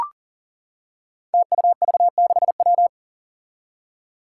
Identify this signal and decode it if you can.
Morse code, heard as a keyed tone, about 40 wpm, 700 Hz. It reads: TUV6W
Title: Five Challenging Letter-Number Combinations 40wpm